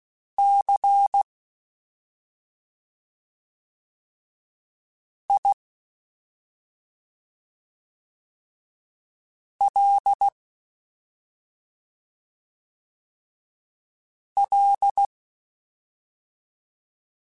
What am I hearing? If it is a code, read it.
Morse code, heard as a keyed tone, about 16 wpm, spaced as for 2 wpm, 800 Hz. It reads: CILL
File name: Special INGs-mp3s / Special INGs-mp3s / Slow ing @10